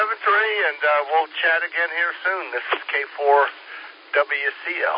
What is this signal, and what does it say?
Emergency Alert System (SAME) tones, heard as harsh buzzing bursts, with speech. It is heard from 0.4 to 4.1 s.